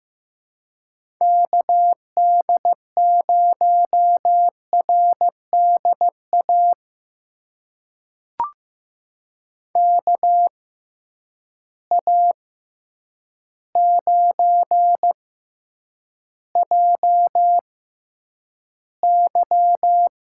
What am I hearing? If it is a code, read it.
Morse code, heard as a keyed tone, about 15 wpm, spaced as for 8 wpm, 700 Hz. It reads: KD0RDA KA9JY
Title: USA Call Signs - 6 Characters - 6x Character Spacing - 15wpm